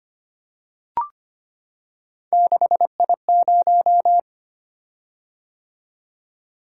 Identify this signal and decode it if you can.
Morse code, heard as a keyed tone, about 25 wpm, 700 Hz. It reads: 6I0
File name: Three Letter-Number Combinations - 25wpm